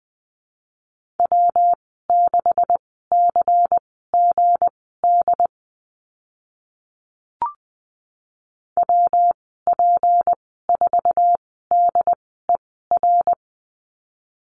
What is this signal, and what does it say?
Morse code, heard as a keyed tone, about 20 wpm, spaced as for 14 wpm, 700 Hz. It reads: W6CGD WP4DER
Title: 161 Lesson 26g - Letter g - Call Signs 20wpm